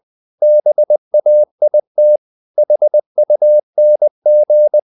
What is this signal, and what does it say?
Morse code, heard as a keyed tone, about 20 wpm, 600 Hz. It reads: BAIT HUNG
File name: the old man and the sea / the old man and the sea part 1 20wpm